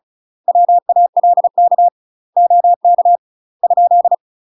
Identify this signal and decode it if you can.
Morse code, heard as a keyed tone, about 33 wpm, 700 Hz. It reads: WALK OK ?